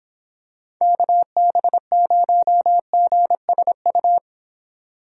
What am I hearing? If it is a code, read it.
Morse code, heard as a keyed tone, about 26 wpm, 700 Hz. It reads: KB0GSU